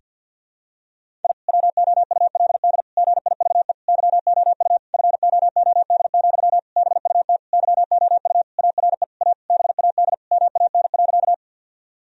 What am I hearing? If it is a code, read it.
Morse code, heard as a keyed tone, about 52 wpm, 700 Hz. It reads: I WOULD GIVE YOU FOOD, BUT YOU ARE A BAD MAN.
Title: Sentences from Top 1000 Words - Review Part 1 - 50wpm